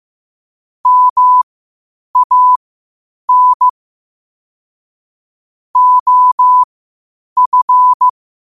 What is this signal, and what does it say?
Morse code, heard as a keyed tone, about 15 wpm, spaced as for 8 wpm, 1000 Hz. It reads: MAN OF
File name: qotdfiles / 8wpm